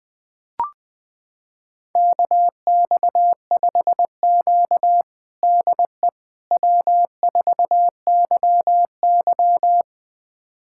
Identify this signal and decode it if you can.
Morse code, heard as a keyed tone, about 21 wpm, 700 Hz. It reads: KX5Q DE W4YY
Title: USA Callsign DE Callsign - 4 Characters - 20wpm